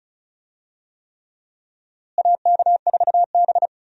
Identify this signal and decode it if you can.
Morse code, heard as a keyed tone, about 34 wpm, 700 Hz. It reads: AK4B
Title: USA Call Signs - 4 Characters - Repeated 3x - 35wpm